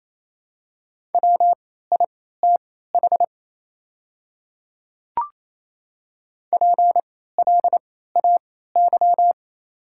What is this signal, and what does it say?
Morse code, heard as a keyed tone, about 28 wpm, spaced as for 16 wpm, 700 Hz. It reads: WITH PLAY
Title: 3x Character Spacing - Top 300 Words - 28wpm